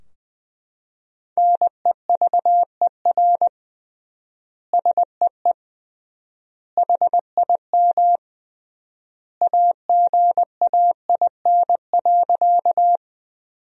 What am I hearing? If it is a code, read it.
Morse code, heard as a keyed tone, about 20 wpm, 700 Hz. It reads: NEVER SEE HIM AGAIN.
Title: 3x Word-Spacing - Sentences from Top 300 Words - 20wpm